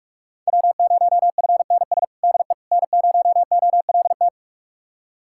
Morse code, heard as a keyed tone, about 42 wpm, 700 Hz. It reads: W0FNS DE N0OLT
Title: USA Callsign DE Callsign - 5 Characters - 45wpm